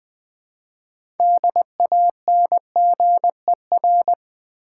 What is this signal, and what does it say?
Morse code, heard as a keyed tone, about 20 wpm, 700 Hz. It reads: DANGER
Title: 6 Letter Words - First 3 Letters Given - 20wpm